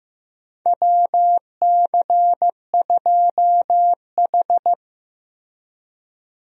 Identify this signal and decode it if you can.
Morse code, heard as a keyed tone, about 15 wpm, 700 Hz. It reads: WC2H
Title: USA Call Signs - 4 Characters - Repeated 2x - 15wpm